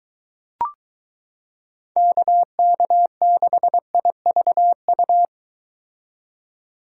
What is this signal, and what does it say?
Morse code, heard as a keyed tone, about 22 wpm, 700 Hz. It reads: KK6IVU